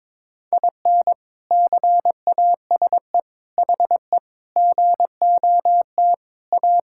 Morse code, heard as a keyed tone, about 22 wpm, 700 Hz. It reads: IN CASE HE GOT A
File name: All-Age Appropriate Jokes - 22wpm